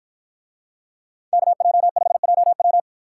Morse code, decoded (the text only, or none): KJ5JW